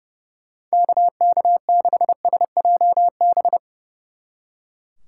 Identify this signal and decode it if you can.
Morse code, heard as a keyed tone, about 39 wpm, 700 Hz. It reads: KK6SJB